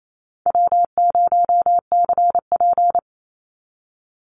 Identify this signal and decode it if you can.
Morse code, heard as a keyed tone, about 28 wpm, 700 Hz. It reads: W0CP
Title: Easy Call Signs Repeated 3x - 28wpm